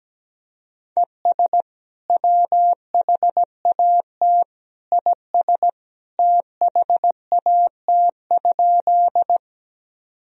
Morse code, heard as a keyed tone, about 17 wpm, 700 Hz. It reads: ES WHAT IS THAT?
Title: Sentences from Top 100 Words and Abbreviations 17wpm